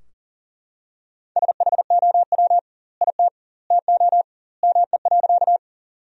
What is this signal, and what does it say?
Morse code, heard as a keyed tone, about 40 wpm, 700 Hz. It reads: SHOW IT TO ME.
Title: 2x Word-Spacing - Sentences from Top 200 Words - 40wpm